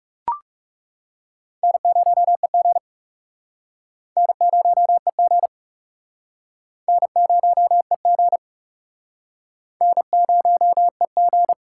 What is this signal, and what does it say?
Morse code, heard as a keyed tone, about 37 wpm, 700 Hz. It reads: N0EG N0EG N0EG N0EG
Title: Speed Racing - USA 1x2 Call Signs 30wpm